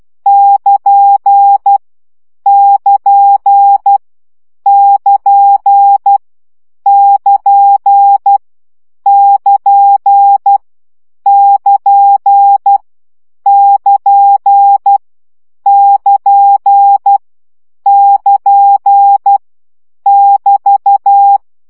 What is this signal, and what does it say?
Morse code, heard as a keyed tone, about 12 wpm, 800 Hz. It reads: <KN> <KN> <KN> <KN> <KN> <KN> <KN> <KN> <KN> <BT>
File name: CW gyakorlas nyito 10 baud